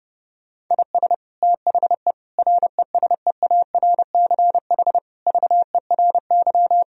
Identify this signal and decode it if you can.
Morse code, heard as a keyed tone, about 32 wpm, 700 Hz. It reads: IS THE RESEARCH VERY
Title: Sentences from Top 1000 Words - Review Part 4 - 30wpm